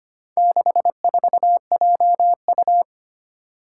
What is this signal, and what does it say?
Morse code, heard as a keyed tone, about 25 wpm, 700 Hz. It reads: 64JU
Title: Four Challenging Letter-Number Combinations 25wpm